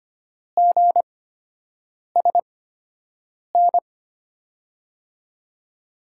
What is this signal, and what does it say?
Morse code, heard as a keyed tone, about 25 wpm, spaced as for 7 wpm, 700 Hz. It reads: GSN
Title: USA Call Signs - 5 Characters - 8x Character Spacing - 25wpm